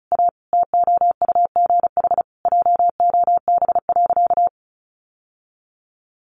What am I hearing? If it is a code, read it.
Morse code, heard as a keyed tone, about 34 wpm, 700 Hz. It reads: A TOUGH JOB.
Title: Sentences from Top 1200 Words 35wpm